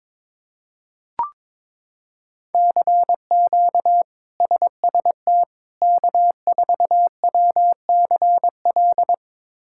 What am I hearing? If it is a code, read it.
Morse code, heard as a keyed tone, about 22 wpm, 700 Hz. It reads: CQ SST K4WCL
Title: K1USN SST QSO Practice - 22wpm